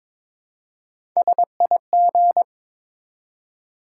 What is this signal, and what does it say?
Morse code, heard as a keyed tone, about 21 wpm, 700 Hz. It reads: SIG
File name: Top 90 QSO Elements - 4x Character Spacing - 22wpm